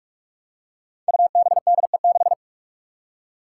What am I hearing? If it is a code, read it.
Morse code, heard as a keyed tone, about 42 wpm, 700 Hz. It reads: UBDE6
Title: Five Challenging Letter-Number Combinations 45wpm